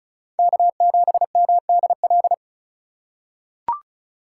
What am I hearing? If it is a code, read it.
Morse code, heard as a keyed tone, about 36 wpm, 700 Hz. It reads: K7MDL
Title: USA Call Signs - 5 Characters - 4x Character Spacing - 35wpm